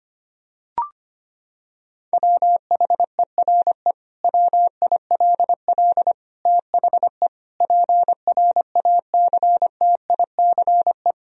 Morse code, heard as a keyed tone, about 25 wpm, 700 Hz. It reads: WHERE WILL THE PRACTICE
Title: Sentences from Top 1000 Words - Review Part 4 - 25wpm